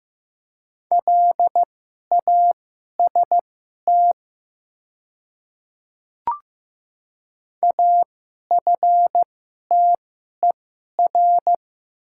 Morse code, heard as a keyed tone, about 15 wpm, spaced as for 11 wpm, 700 Hz. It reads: LAST AFTER
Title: 2x Character Spacing - Top 300 Words - 15wpm